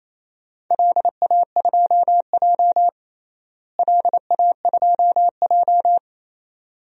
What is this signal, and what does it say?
Morse code, heard as a keyed tone, about 28 wpm, 700 Hz. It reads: LA2J LA2J
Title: DXCC-100 Call Signs - 4 Characters - Repeated 2x - 28wpm